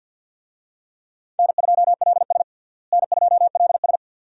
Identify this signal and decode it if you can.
Morse code, heard as a keyed tone, about 49 wpm, 700 Hz. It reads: N2LS N2LS
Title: USA Call Signs - 4 Characters - Repeated 2x - 50wpm